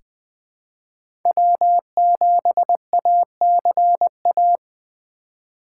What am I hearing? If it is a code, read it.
Morse code, heard as a keyed tone, about 20 wpm, 700 Hz. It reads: W7ACA
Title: USA Call Signs - 5 Characters - 3x Character Spacing - 20wpm